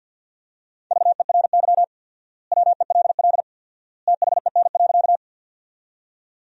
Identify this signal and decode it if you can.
Morse code, heard as a keyed tone, about 46 wpm, 700 Hz. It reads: VERY WELL THEN.
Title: 4x Word-Spacing - Sentences from Top 200 Words - 50wpm